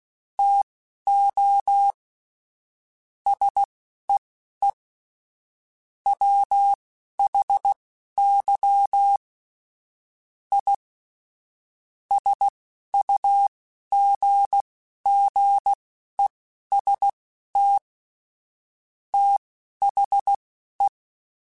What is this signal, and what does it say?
Morse code, heard as a keyed tone, about 16 wpm, spaced as for 11 wpm, 800 Hz. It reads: TO SEE WHY I SUGGEST THE
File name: Technical Subjects / Technical Subjects-mp3s / OR Gate @10